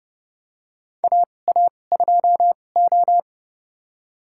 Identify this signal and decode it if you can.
Morse code, heard as a keyed tone, about 32 wpm, spaced as for 22 wpm, 700 Hz. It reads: AA2O
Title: USA Call Signs - 4 Characters - 2x Character Spacing - 30wpm